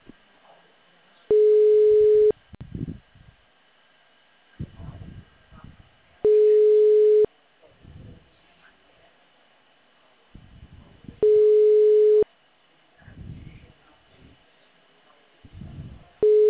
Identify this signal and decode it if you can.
Ringback tone (Europe, 425 Hz).